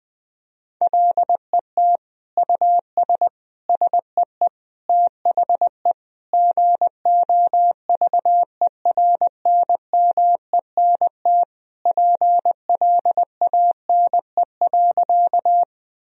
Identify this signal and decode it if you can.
Morse code, heard as a keyed tone, about 20 wpm, 700 Hz. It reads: LET US SEE THE GOVERNMENT PLANE.
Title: Sentences from Top 1000 Words - Review Part 3 - 20wpm